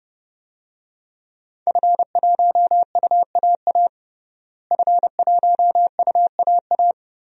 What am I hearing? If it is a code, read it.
Morse code, heard as a keyed tone, about 30 wpm, 700 Hz. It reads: F1UAA F1UAA
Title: DXCC-100 Call Signs - 5 Characters - Repeated 3x - 30wpm